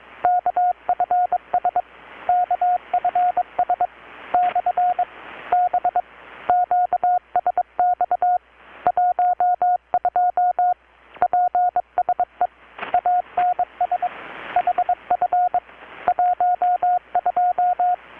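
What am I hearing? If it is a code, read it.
Morse code, heard as a keyed tone, about 21 wpm, 720 Hz, with radio noise. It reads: KFS KFS / B QSX 12 PSE ANS HF 12
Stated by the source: Morse from KPH's Night of Nights XX maritime radiotelegraphy event, received in Chiapas, Mexico about 3,800 km away on a Kenwood R-600 with a grounded 10 m long-wire antenna.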